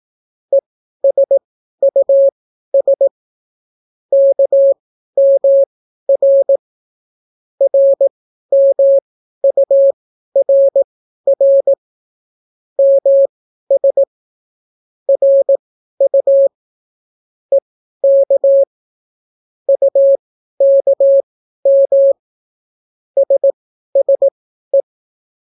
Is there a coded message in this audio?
Yes: Morse code from 0.5 to 24.8 s, heard as a keyed tone, about 18 wpm, spaced as for 12 wpm, 550 Hz. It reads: ESUS KMR RMURR MS RU EK UKM SSE